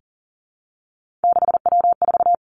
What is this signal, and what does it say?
Morse code, heard as a keyed tone, about 38 wpm, 700 Hz. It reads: 6W4
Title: Three Challenging Letter-Number Combinations 40wpm